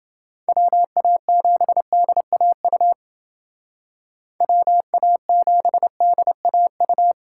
Morse code, heard as a keyed tone, about 29 wpm, 700 Hz. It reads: WA7DAU WA7DAU